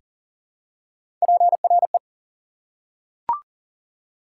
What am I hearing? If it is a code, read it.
Morse code, heard as a keyed tone, about 40 wpm, 700 Hz. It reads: PRE
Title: Prefixes and Suffixes 40wpm